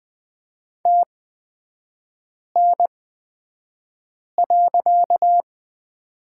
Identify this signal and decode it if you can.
Morse code, heard as a keyed tone, about 20 wpm, spaced as for 5 wpm, 700 Hz. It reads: TN.
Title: 255 Lesson 41e - Period - Three Characters 20wpm